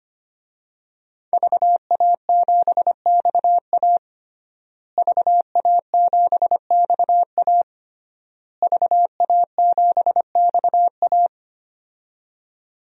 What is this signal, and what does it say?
Morse code, heard as a keyed tone, about 25 wpm, 700 Hz. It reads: VA7XA VA7XA VA7XA
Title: DXCC-100 Call Signs - 5 Characters - Repeated 3x - 25wpm